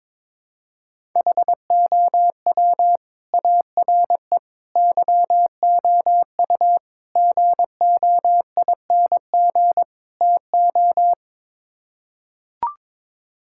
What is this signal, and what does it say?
Morse code, heard as a keyed tone, about 22 wpm, 700 Hz. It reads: HOW ARE YOU GOING TO
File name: practice / 22wpm / Build A Phrase - 5-Word Phrases - 22wpm